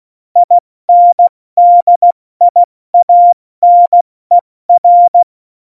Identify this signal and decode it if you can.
Morse code, heard as a keyed tone, about 16 wpm, 700 Hz. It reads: INDIANER
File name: Morsekurs / Witze / Witz 06. Oktober 2020, 70 F80